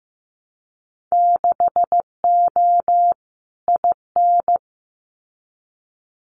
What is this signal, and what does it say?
Morse code, heard as a keyed tone, about 15 wpm, 700 Hz. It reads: 6O IN
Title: Winter Field Day Exchanges Encore 15wpm